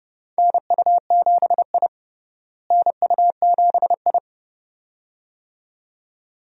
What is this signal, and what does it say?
Morse code, heard as a keyed tone, about 30 wpm, 700 Hz. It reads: NU7I NU7I
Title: USA Call Signs - 4 Characters - Repeated 2x - 30wpm